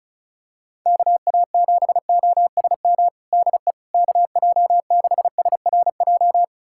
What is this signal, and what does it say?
Morse code, heard as a keyed tone, about 36 wpm, 700 Hz. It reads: KA7OSM DE KJ6SRJ
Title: USA Callsign DE Callsign 35wpm